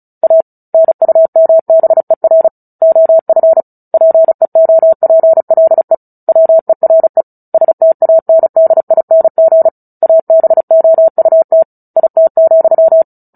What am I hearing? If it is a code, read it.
Morse code, heard as a keyed tone, about 34 wpm, 650 Hz. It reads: A NUMBER OF PEOPLE WERE STANDING ABOUT IT,
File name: Book Two / 09-Chapter 9 - 35 WPM